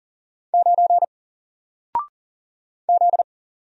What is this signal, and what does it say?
Morse code, heard as a keyed tone, about 40 wpm, 700 Hz. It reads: 9 Z
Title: Single Character (Letter:Number:Punctuation:Prosign) Alt 40wpm